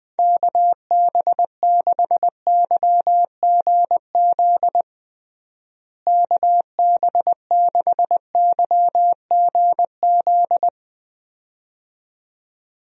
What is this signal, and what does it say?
Morse code, heard as a keyed tone, about 20 wpm, 700 Hz. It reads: KB6YGZ KB6YGZ
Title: USA Call Signs - 6 Characters - Repeated 2x - 20wpm